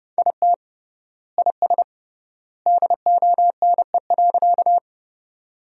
Morse code, heard as a keyed tone, about 32 wpm, 700 Hz. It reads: IT IS DONE.
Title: 3x Word-Spacing - Sentences from Top 500 Words - 30wpm